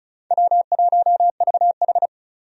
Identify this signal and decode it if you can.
Morse code, heard as a keyed tone, about 36 wpm, 700 Hz. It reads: W1VH
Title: USA Call Signs - 4 Characters - Repeated 3x - 35wpm